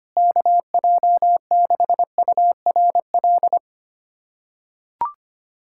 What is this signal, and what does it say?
Morse code, heard as a keyed tone, about 25 wpm, 700 Hz. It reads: KJ6URL
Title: USA Callsign DE Callsign - 6 Characters - 25wpm